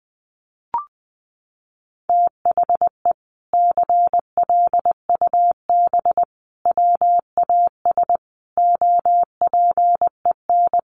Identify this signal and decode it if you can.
Morse code, heard as a keyed tone, about 20 wpm, 700 Hz. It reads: THE CLUB WAS OPEN
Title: Sentences from Top 1200 Words 20wpm